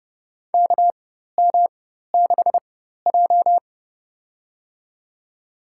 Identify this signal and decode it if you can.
Morse code, heard as a keyed tone, about 32 wpm, spaced as for 14 wpm, 700 Hz. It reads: KM6J
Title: USA Call Signs - 4x Character Spacing - 30wpm